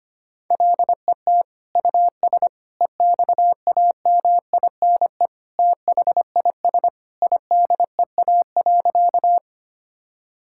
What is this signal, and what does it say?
Morse code, heard as a keyed tone, about 25 wpm, 700 Hz. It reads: LET US EXAMINE THIS IDEA.
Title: Sentences from Top 1600 Words 25wpm